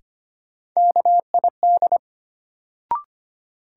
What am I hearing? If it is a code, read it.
Morse code, heard as a keyed tone, about 25 wpm, 700 Hz. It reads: KID
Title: Sentences from Top 1600 Words - New Word Drill Alt - 25wpm